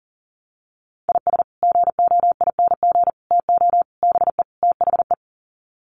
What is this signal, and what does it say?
Morse code, heard as a keyed tone, about 40 wpm, 700 Hz. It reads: IS GOING TO BE THE
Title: Sets of 5 Words 40wpm